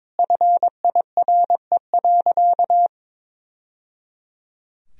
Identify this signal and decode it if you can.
Morse code, heard as a keyed tone, about 22 wpm, 700 Hz. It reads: FIRE.
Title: Sentences from Top 1000 Words - Review Part 2 - 22wpm